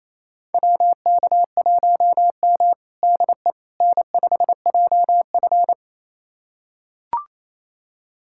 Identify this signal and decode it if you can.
Morse code, heard as a keyed tone, about 28 wpm, 700 Hz. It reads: WK1M DE N5JF
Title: USA Callsign DE Callsign - 4 Characters - 28wpm